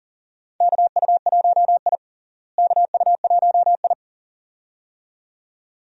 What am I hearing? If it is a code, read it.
Morse code, heard as a keyed tone, about 40 wpm, 700 Hz. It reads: KU1I KU1I
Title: USA Call Signs - 4 Characters - Repeated 2x - 40wpm